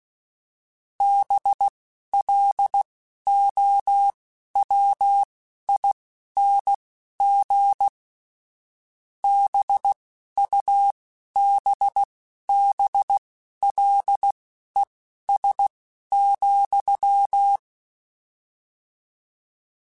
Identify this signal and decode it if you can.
Morse code, heard as a keyed tone, about 16 wpm, spaced as for 11 wpm, 800 Hz. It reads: BLOWING BUBBLES,
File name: Misc / Misc-mp3s / BIG @10